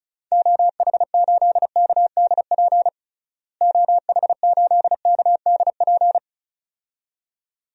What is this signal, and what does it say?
Morse code, heard as a keyed tone, about 34 wpm, 700 Hz. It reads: OH8KDP OH8KDP
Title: DXCC-100 Call Signs - 6 Characters - Repeated 2x - 35wpm